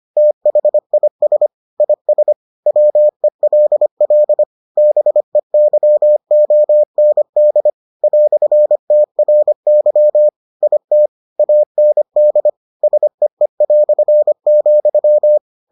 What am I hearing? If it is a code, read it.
Morse code, heard as a keyed tone, about 25 wpm, 600 Hz. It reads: THIS IS WELL BEYOND "TRY IT AND SEE",